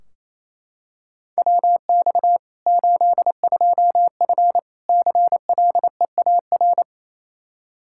Morse code, heard as a keyed tone, about 28 wpm, 700 Hz. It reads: WX 82F CLEAR
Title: Practice QSO - Exchange 2 - Info - 28wpm